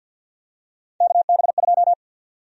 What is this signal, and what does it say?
Morse code, heard as a keyed tone, about 51 wpm, 700 Hz. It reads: KB2
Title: USA Call Sign Prefixes - Repeated 2x - 50wpm